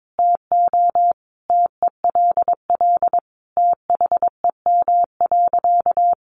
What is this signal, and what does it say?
Morse code, heard as a keyed tone, about 22 wpm, 700 Hz. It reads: TO TELL THEM.
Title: Sentences from Top 700 Words 22wpm